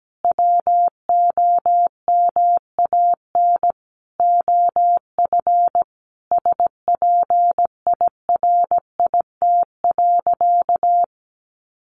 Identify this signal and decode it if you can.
Morse code, heard as a keyed tone, about 17 wpm, 700 Hz. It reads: WOMAN OF SPIRIT.
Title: Sentences from Top 1400 Words 17wpm